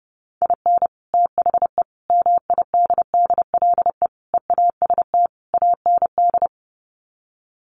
Morse code, heard as a keyed tone, about 30 wpm, 700 Hz. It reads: IN THE MIDDLE EAST AND
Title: Sets of 5 Words Encore 30wpm